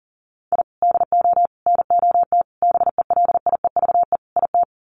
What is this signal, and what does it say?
Morse code, heard as a keyed tone, about 40 wpm, 700 Hz. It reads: I DO NOT BELIEVE IT
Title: Sets of 5 Words 40wpm